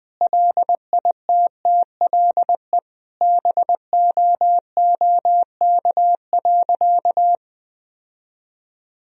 Morse code, heard as a keyed tone, about 20 wpm, 700 Hz. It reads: LITTLE BOOK.
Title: Sentences from Top 1000 Words - Review Part 2 - 20wpm